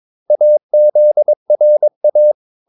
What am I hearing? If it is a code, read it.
Morse code, heard as a keyed tone, about 22 wpm, 600 Hz. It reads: AZRA
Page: Romanen Never af Ken Follet på engelsk.